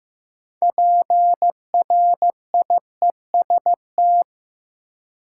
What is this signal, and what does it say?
Morse code, heard as a keyed tone, about 15 wpm, 700 Hz. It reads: PRIEST
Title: Six Letter Words - 15wpm